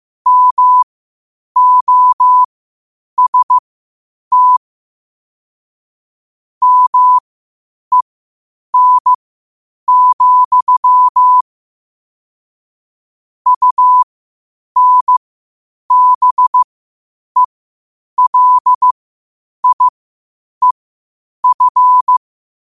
Morse code, heard as a keyed tone, about 15 wpm, spaced as for 8 wpm, 1000 Hz. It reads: MOST MEN, UNBELIEF
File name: qotdfiles / 8wpm